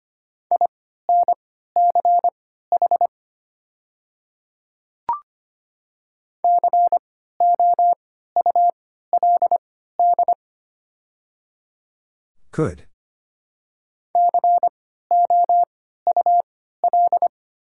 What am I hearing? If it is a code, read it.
Morse code, heard as a keyed tone, about 25 wpm, spaced as for 14 wpm, 700 Hz. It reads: INCH COULD COUL
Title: 3x Character Spacing - Top 500 Words - 25wpm